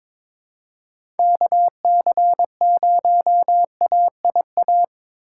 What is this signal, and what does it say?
Morse code, heard as a keyed tone, about 22 wpm, 700 Hz. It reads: KC0AIA
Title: USA Call Signs - 2x Character Spacing - 22wpm